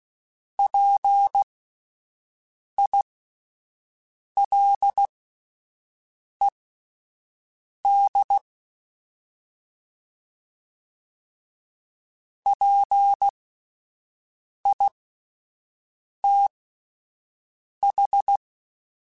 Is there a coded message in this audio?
Yes: Morse code from 0.6 to 18.4 s, heard as a keyed tone, about 16 wpm, spaced as for 5 wpm, 800 Hz. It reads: PILED PITH